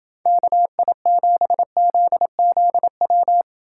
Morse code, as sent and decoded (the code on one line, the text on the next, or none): -.- .. --... --.. --.. .--
KI7ZZW